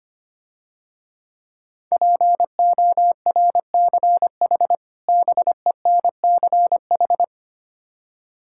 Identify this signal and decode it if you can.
Morse code, heard as a keyed tone, about 25 wpm, 700 Hz. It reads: PORCH BENCH
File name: practice / 25wpm / Difficult Words - Speed-Racing with Hints - 25wpm